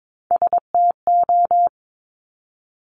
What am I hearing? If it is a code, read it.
Morse code, heard as a keyed tone, about 22 wpm, 700 Hz. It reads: STO
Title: Top Three Letter Combinations in Words 22wpm